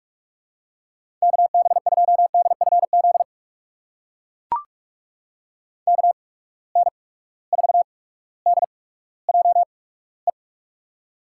Morse code, heard as a keyed tone, about 44 wpm, spaced as for 19 wpm, 700 Hz. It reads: KB2DFZ KN4DJE
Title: USA Call Signs - 6 Characters - 8x Character Spacing - 45wpm